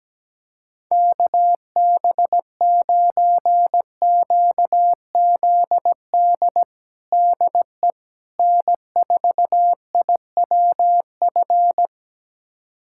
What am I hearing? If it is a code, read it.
Morse code, heard as a keyed tone, about 17 wpm, 700 Hz. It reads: KB9QZD DE N4IWF
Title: USA Callsign DE Callsign 17wpm